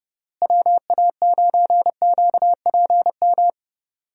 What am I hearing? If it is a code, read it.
Morse code, heard as a keyed tone, about 30 wpm, 700 Hz. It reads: WA9QPM
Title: USA Call Signs - 6 Characters - Repeated 3x - 30wpm